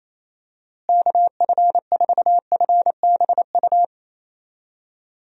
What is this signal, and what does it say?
Morse code, heard as a keyed tone, about 28 wpm, 700 Hz. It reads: KF4FBU
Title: USA Call Signs - 6 Characters - Repeated 2x - 28wpm